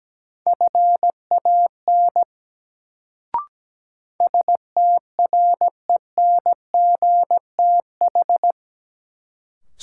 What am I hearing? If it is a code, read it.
Morse code, heard as a keyed tone, about 17 wpm, 700 Hz. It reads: FAN STRENGTH
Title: Sentences from Top 1000 Words - Review Part 4 - New Word Drill - 17wpm